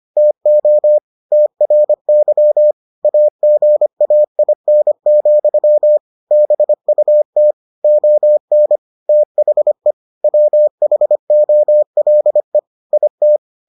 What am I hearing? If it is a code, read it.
Morse code, heard as a keyed tone, about 25 wpm, 600 Hz. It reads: TO TRY AGAIN, BUT ON THE WHOLE IT